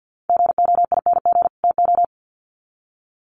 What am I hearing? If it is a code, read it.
Morse code, heard as a keyed tone, about 48 wpm, 700 Hz. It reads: GOING TO
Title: Sets of 2 Words 50wpm